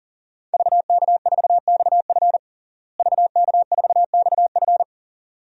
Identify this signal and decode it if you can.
Morse code, heard as a keyed tone, about 40 wpm, 700 Hz. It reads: VK4XF VK4XF
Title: DXCC-100 Call Signs - 5 Characters - Repeated 2x - 40wpm